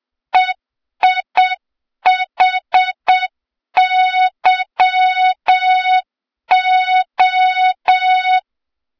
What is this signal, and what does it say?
Morse code, heard as a keyed tone, about 7 wpm, 3700 Hz, with noise. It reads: EIHYO
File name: Morze-HA2MN / hang / 23 GYAKORLAT